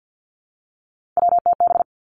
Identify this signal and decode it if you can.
Morse code, heard as a keyed tone, about 50 wpm, 700 Hz. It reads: WTB
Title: Three Challenging Letter-Number Combinations 50wpm